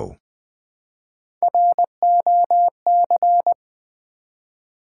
Morse code, heard as a keyed tone, about 20 wpm, 700 Hz. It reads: ROC
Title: Build A Word - 1 Letter At A Time - 4-Letter Words - 20wpm